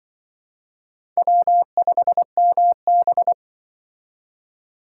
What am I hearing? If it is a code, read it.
Morse code, heard as a keyed tone, about 24 wpm, 700 Hz. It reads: W5MB
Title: Speed Racing - USA Call Signs 20wpm